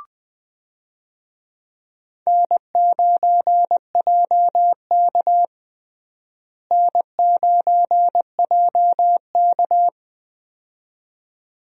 Morse code, heard as a keyed tone, about 20 wpm, 700 Hz. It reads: N9JK N9JK
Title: USA Call Signs - 4 Characters - Repeated 2x - 20wpm